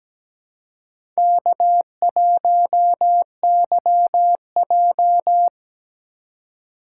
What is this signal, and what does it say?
Morse code, heard as a keyed tone, about 17 wpm, 700 Hz. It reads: K1YJ
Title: USA Call Signs - 4 Characters - 4x Character Spacing - 17wpm